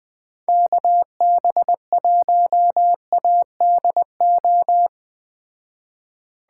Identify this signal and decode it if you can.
Morse code, heard as a keyed tone, about 20 wpm, 700 Hz. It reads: KB1ADO